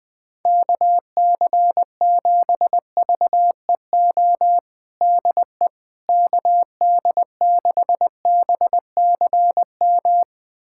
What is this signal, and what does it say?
Morse code, heard as a keyed tone, about 21 wpm, 700 Hz. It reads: KC7VEO DE KD6BCM
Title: USA Callsign DE Callsign - 6 Characters - 20wpm